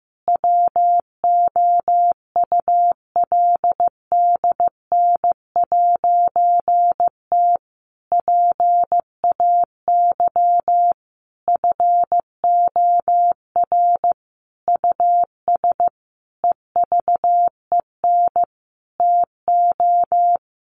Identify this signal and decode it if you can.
Morse code, heard as a keyed tone, about 15 wpm, 700 Hz. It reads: WOULDN'T PAY FOR US EVEN TO
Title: A Sound Of Thunder 15wpm